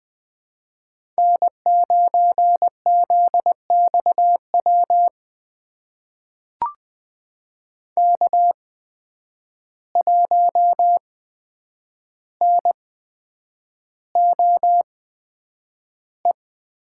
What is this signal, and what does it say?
Morse code, heard as a keyed tone, about 20 wpm, spaced as for 9 wpm, 700 Hz. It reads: N9ZXW K1NOE
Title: USA Call Signs - 5 Characters - 8x Character Spacing - 20wpm